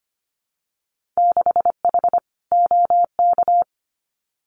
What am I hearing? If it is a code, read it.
Morse code, heard as a keyed tone, about 25 wpm, 690 Hz. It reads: 6H OK
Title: Winter Field Day Exchanges Encore 25wpm